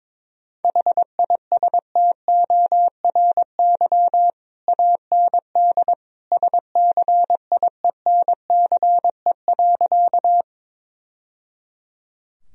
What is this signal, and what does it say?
Morse code, heard as a keyed tone, about 22 wpm, 700 Hz. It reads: HISTORY AND SCIENCE.
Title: Sentences from Top 1000 Words - Review Part 2 - 22wpm